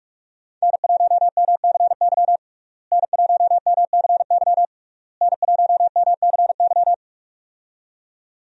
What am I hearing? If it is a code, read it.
Morse code, heard as a keyed tone, about 41 wpm, 700 Hz. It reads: N1MCY N1MCY N1MCY
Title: USA Call Signs - 5 Characters - Repeated 3x - 45wpm